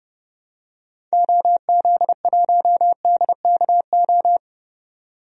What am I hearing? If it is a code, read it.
Morse code, heard as a keyed tone, about 29 wpm, 690 Hz. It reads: OZ1DKO
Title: DXCC-100 Call Signs - 6 Characters - 30wpm